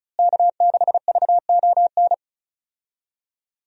Morse code, heard as a keyed tone, about 34 wpm, 700 Hz. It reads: K6VON
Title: USA Call Signs - 5 Characters - Repeated 3x - 35wpm